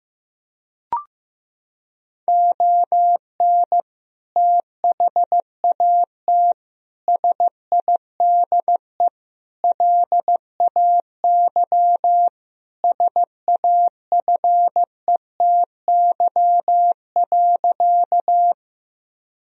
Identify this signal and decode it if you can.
Morse code, heard as a keyed tone, about 16 wpm, 700 Hz. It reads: ON THAT SIDE LAY SAFETY.
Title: Sentences from Top 1000 Words - Review Part 4 - 15wpm